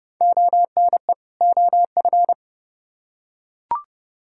Morse code, heard as a keyed tone, about 30 wpm, 700 Hz. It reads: ONE OF